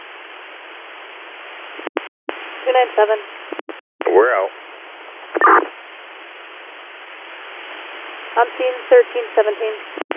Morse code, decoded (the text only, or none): none